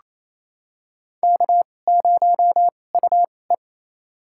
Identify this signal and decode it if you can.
Morse code, heard as a keyed tone, about 28 wpm, spaced as for 20 wpm, 700 Hz. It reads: K0UE
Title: USA Call Signs - 4 Characters - 2x Character Spacing - 28wpm